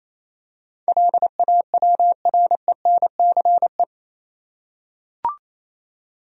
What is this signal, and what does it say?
Morse code, heard as a keyed tone, about 28 wpm, 700 Hz. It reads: LAWRENCE
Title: USA Common Names - 28wpm